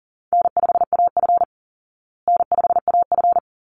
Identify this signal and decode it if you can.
Morse code, heard as a keyed tone, about 40 wpm, 700 Hz. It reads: N5AF N5AF
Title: Easy Call Signs Repeated 2x - 40wpm